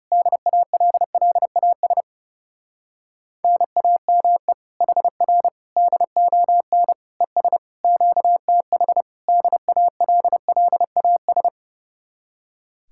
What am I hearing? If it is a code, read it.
Morse code, heard as a keyed tone, about 28 wpm, 700 Hz. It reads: DALLAS NAME HR DON ES QTH DALLAS
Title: Speed-Racing Common QSO Phrases Encore - 30wpm